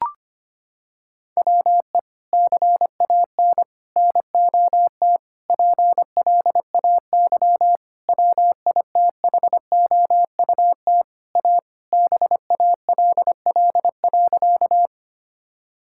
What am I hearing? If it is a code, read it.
Morse code, heard as a keyed tone, about 25 wpm, 700 Hz. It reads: WE CAN NOT PLAY WITHOUT A BALL.
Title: Sentences from Top 1000 Words - Review Part 6 - 25wpm